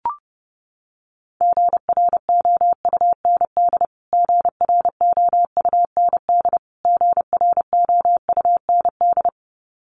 Morse code, heard as a keyed tone, about 30 wpm, 700 Hz. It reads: GROUND GROUND GROUND
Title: Top 500 Words Repeated 30wpm